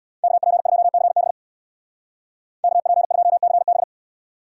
Morse code, heard as a keyed tone, about 65 wpm, 700 Hz. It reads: KP2CB KP2CB